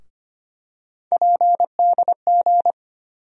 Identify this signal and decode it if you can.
Morse code, heard as a keyed tone, about 25 wpm, 700 Hz. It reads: PDG
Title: USA Call Sign Postfixes - 3 Letters 25wpm